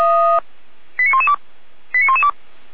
Five-tone selective call (ZVEI), heard as quick runs of five tones, tones of about 80 ms. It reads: 89112 89112